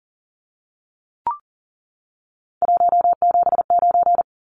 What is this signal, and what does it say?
Morse code, heard as a keyed tone, about 40 wpm, 700 Hz. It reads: 179
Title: Numbers - 3-Digit 40wpm